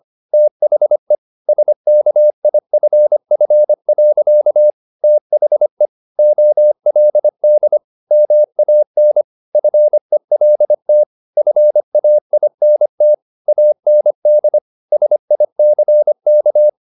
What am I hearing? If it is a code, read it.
Morse code, heard as a keyed tone, about 25 wpm, 600 Hz. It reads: THE SKIFF. THE OLD MAN FELT FAINT AND SICK